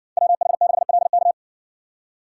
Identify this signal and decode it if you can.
Morse code, heard as a keyed tone, about 60 wpm, 700 Hz. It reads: WH6LK